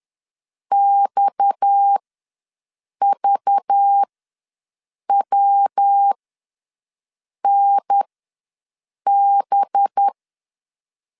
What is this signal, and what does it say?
Morse code, heard as a keyed tone, about 10 wpm, spaced as for 6 wpm, 800 Hz. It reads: XVWNB
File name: LEZIONI / 7 GF9Z / 7 GF9 ws 10wpm 4034